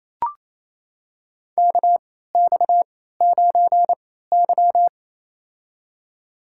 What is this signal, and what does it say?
Morse code, heard as a keyed tone, about 28 wpm, spaced as for 16 wpm, 700 Hz. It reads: KX9Y
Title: USA Call Signs - 4 Characters - 3x Character Spacing - 28wpm